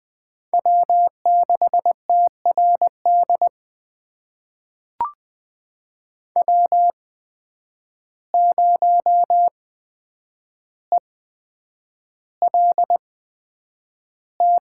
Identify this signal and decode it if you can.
Morse code, heard as a keyed tone, about 20 wpm, spaced as for 9 wpm, 700 Hz. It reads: W6TRD W0ELT
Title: USA Call Signs - 8x Character Spacing - 20wpm